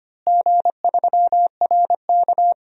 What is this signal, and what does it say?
Morse code, heard as a keyed tone, about 25 wpm, 700 Hz. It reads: G3RK
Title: DXCC-100 Call Signs - 4 Characters - Repeated 2x - 25wpm